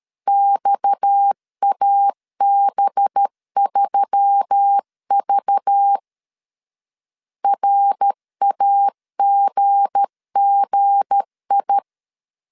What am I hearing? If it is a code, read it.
Morse code, heard as a keyed tone, about 13 wpm, 800 Hz. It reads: XAB3V RAGGI
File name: LEZIONI / 7 GF9Z / 7 G 12wpm 2055